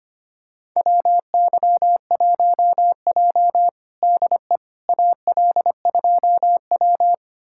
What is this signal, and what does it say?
Morse code, heard as a keyed tone, about 25 wpm, 700 Hz. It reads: WY1J DE AL2W
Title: USA Callsign DE Callsign - 4 Characters - 25wpm